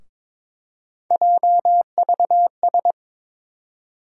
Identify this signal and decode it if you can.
Morse code, heard as a keyed tone, about 21 wpm, 700 Hz. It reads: JVS
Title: USA Call Sign Postfixes - 3 Letters 22wpm